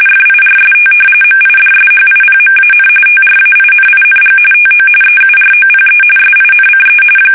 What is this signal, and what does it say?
RTTY, about 45.45 baud. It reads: FIRST MIDSHIPMAN TRAINING TO EUROPE SINCE
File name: fsk850 / battleshipnj